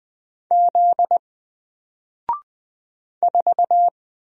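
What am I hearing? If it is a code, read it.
Morse code, heard as a keyed tone, about 20 wpm, 700 Hz. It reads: Z 4
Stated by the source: Single letter and number Morse drill at 20 wpm, from N4LCD.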